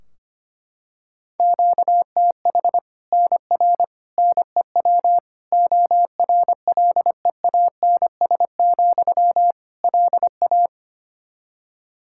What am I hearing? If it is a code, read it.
Morse code, heard as a keyed tone, about 25 wpm, 700 Hz. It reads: QTH NR NEW ORLEANS, LA
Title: K1USN SST QSO Practice 25wpm